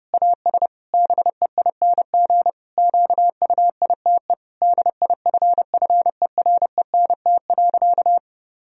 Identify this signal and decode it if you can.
Morse code, heard as a keyed tone, about 30 wpm, 700 Hz. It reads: AS BEING QUITE DIFFERENT.
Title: Sentences from Top 1000 Words - Review Part 3 - 30wpm